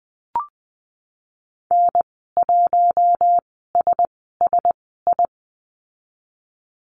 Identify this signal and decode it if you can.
Morse code, heard as a keyed tone, about 20 wpm, spaced as for 14 wpm, 700 Hz. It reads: N1SSI
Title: 042 Lesson 8g - Number 1 - Call Signs 20wpm